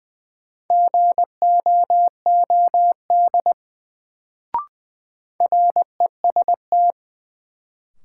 Morse code, heard as a keyed tone, about 20 wpm, 700 Hz. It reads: GOOD REST
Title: Sets of 2 Words - New Word Drill - 20wpm